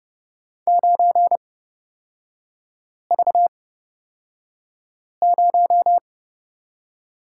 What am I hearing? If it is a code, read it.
Morse code, heard as a keyed tone, about 30 wpm, spaced as for 5 wpm, 700 Hz. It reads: 9V0
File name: Lesson 37e - Letter x - Three Characters 30wpm